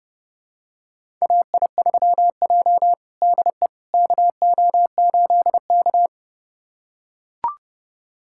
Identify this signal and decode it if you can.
Morse code, heard as a keyed tone, about 30 wpm, 700 Hz. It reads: AI3J DE KO8K
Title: USA Callsign DE Callsign - 4 Characters - 30wpm